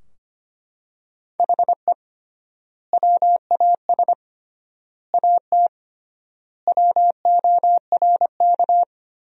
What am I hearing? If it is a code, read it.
Morse code, heard as a keyed tone, about 25 wpm, 700 Hz. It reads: HE WAS AT WORK
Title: 3x Word-Spacing - Sentences from Top 200 Words - 25wpm